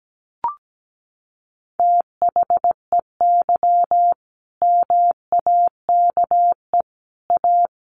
Morse code, heard as a keyed tone, about 17 wpm, 700 Hz. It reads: THEY MAKE A
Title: Sentences from Top 1200 Words 17wpm